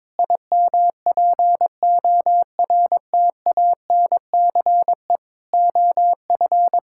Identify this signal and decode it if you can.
Morse code, heard as a keyed tone, about 21 wpm, 700 Hz. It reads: IMPORTANCE OF
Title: Sentences from Top 1600 Words 22wpm